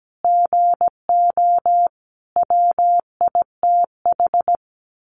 Morse code, heard as a keyed tone, about 17 wpm, 700 Hz. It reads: GO WITH
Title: Sentences from Top 1300 Words 17wpm